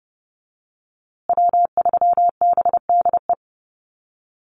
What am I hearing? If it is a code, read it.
Morse code, heard as a keyed tone, about 30 wpm, 700 Hz. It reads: W3BDE
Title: USA Call Signs Repeated 2x 30wpm